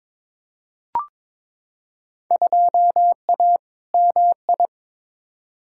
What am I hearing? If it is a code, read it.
Morse code, heard as a keyed tone, about 21 wpm, 700 Hz. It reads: 2A MI
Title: Field Day Exchanges - 22wpm